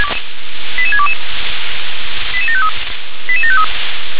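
Five-tone selective call (ZVEI), heard as quick runs of five tones, tones of about 70 ms. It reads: 88622 88753 88753
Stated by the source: ZVEI five-tone selective call received by SDR.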